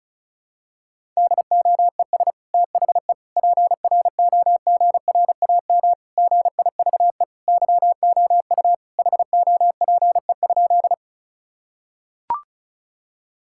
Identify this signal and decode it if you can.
Morse code, heard as a keyed tone, about 36 wpm, 700 Hz. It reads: DOES THE PROGRAM GIVE YOU HOPE?
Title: Sentences from Top 1000 Words - Review Part 5 - 35wpm